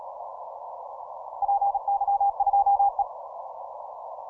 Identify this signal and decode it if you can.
Morse code, heard as a keyed tone, about 36 wpm, 800 Hz. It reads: PY2E
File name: simple-file-list / 8Q7DX 15MCW